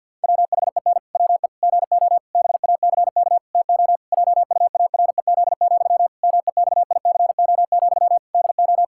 Morse code, heard as a keyed tone, about 49 wpm, 700 Hz. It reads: WHEN WE GO BACK TO JUAREZ, MEXICO, DO
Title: Pangram Challenge v2 - 50wpm